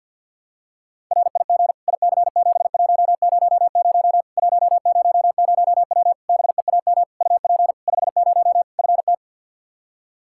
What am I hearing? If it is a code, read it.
Morse code, heard as a keyed tone, about 49 wpm, 700 Hz. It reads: RIG IC7100 100W BEAM UP 50 FT
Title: Practice QSO - Exchange 2 - Info - 50wpm